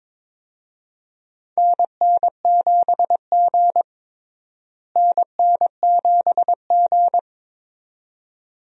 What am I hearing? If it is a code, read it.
Morse code, heard as a keyed tone, about 21 wpm, 700 Hz. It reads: NN7G NN7G
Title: USA Call Signs - 4 Characters - Repeated 2x - 22wpm